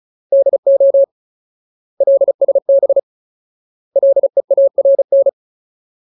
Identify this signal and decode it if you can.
Morse code, heard as a keyed tone, about 34 wpm, 550 Hz. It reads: DO LSB LEARN
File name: Ultimate Word List / 35WPM Ultimate